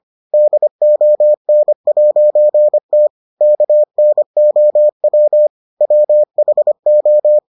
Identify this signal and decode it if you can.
Morse code, heard as a keyed tone, about 25 wpm, 600 Hz. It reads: DON'T KNOW WHO